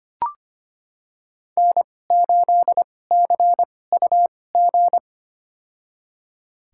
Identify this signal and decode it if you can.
Morse code, heard as a keyed tone, about 25 wpm, spaced as for 18 wpm, 700 Hz. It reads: N8CUG
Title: USA Call Signs - 2x Character Spacing - 25wpm